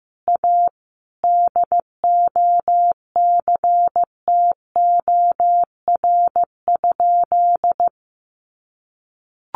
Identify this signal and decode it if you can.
Morse code, heard as a keyed tone, about 15 wpm, 700 Hz. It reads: A DOCTOR?
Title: Sentences from Top 1500 Words 15wpm